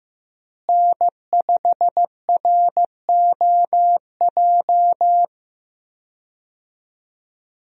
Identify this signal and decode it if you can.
Morse code, heard as a keyed tone, about 15 wpm, 700 Hz. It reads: N5ROJ
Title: USA Call Signs - 5 Characters - Repeated 2x - 15wpm